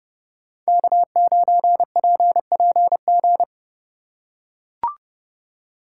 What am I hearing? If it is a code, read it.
Morse code, heard as a keyed tone, about 30 wpm, 700 Hz. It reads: K9PPG
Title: USA Call Signs - 5 Characters - 2x Character Spacing - 30wpm